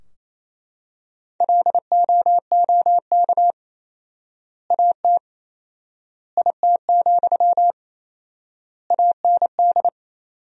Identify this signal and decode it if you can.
Morse code, heard as a keyed tone, about 28 wpm, 700 Hz. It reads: LOOK AT IT, AND
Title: 4x Word-Spacing - Sentences from Top 200 Words - 28wpm